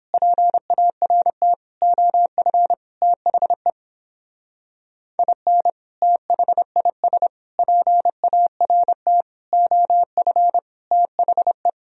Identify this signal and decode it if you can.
Morse code, heard as a keyed tone, about 29 wpm, 700 Hz. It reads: PART OF THE IN THIS PART OF THE